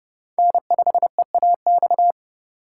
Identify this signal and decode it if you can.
Morse code, heard as a keyed tone, about 31 wpm, 700 Hz. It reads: N5EAX